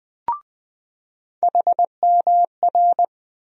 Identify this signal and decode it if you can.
Morse code, heard as a keyed tone, about 20 wpm, 700 Hz. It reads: HMR
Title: USA Call Sign Postfixes - 3 Letters 20wpm